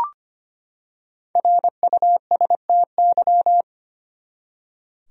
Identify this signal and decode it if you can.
Morse code, heard as a keyed tone, about 25 wpm, 700 Hz. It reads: RUSTY
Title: USA Common Names - Max 5 Letters - 25wpm